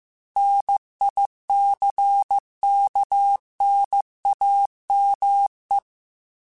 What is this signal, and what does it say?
Morse code, heard as a keyed tone, about 15 wpm, 800 Hz. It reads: NICKNAME
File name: Technical Subjects / Technical Subjects-mp3s / AND Gate @13